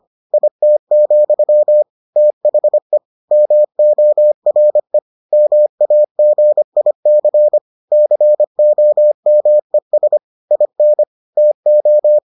Morse code, heard as a keyed tone, about 25 wpm, 600 Hz. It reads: IT, THE MORE MAGIC COMES IN TO